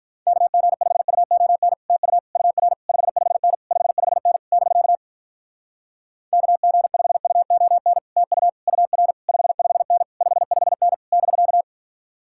Morse code, heard as a keyed tone, about 46 wpm, 700 Hz. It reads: KG5UON TU UR 55N 55N <BK> KG5UON TU UR 55N 55N <BK>